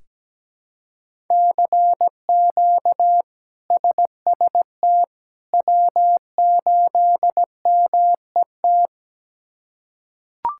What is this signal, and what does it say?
Morse code, heard as a keyed tone, about 17 wpm, 700 Hz. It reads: CQ SST W8MET
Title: K1USN SST QSO Practice - 17wpm